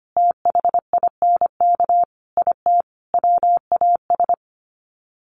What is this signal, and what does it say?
Morse code, heard as a keyed tone, about 25 wpm, 700 Hz. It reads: THINK IT WAS
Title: Sets of 5 Words Encore 25wpm